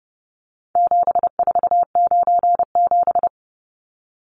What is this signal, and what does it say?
Morse code, heard as a keyed tone, about 30 wpm, 700 Hz. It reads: 7497
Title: Numbers - 4-Digit 30wpm